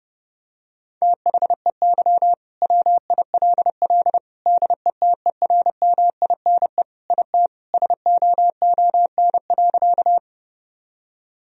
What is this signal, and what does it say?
Morse code, heard as a keyed tone, about 32 wpm, 700 Hz. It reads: THEY WILL DETERMINE IT SOON.
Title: Sentences from Top 1000 Words - Review Part 4 - 30wpm